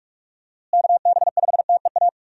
Morse code, heard as a keyed tone, about 47 wpm, 700 Hz. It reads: KB5TEA
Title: USA Call Signs - 6 Characters - Repeated 3x - 45wpm